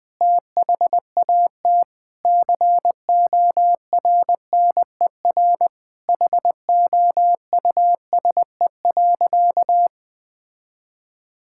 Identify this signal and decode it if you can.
Morse code, heard as a keyed tone, about 20 wpm, 700 Hz. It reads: THAT CORNER HOUSE.
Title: Sentences from Top 1600 Words 20wpm